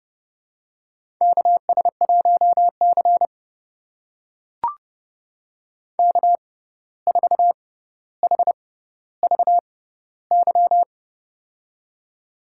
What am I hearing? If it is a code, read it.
Morse code, heard as a keyed tone, about 30 wpm, spaced as for 14 wpm, 700 Hz. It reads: KS1C K4HVY
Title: USA Call Signs - 6x Character Spacing - 30wpm